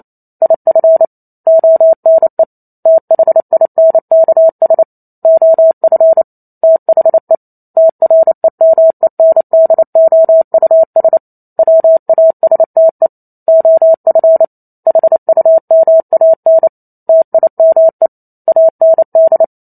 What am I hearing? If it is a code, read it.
Morse code, heard as a keyed tone, about 28 wpm, 650 Hz. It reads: IF ONE THINKS OF THE TREMENDOUS WASTE OF HUMAN TIME AND